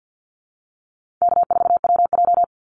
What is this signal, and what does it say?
Morse code, heard as a keyed tone, about 50 wpm, 700 Hz. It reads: K4WJ
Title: Call Signs - 50wpm